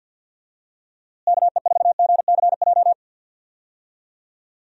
Morse code, heard as a keyed tone, about 51 wpm, 700 Hz. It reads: KE4GCJ
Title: USA Call Signs - 6 Characters - Repeated 3x - 50wpm